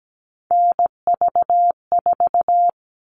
Morse code, heard as a keyed tone, about 17 wpm, 700 Hz. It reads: NV4
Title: All Cut Numbers - 3-Digit 17wpm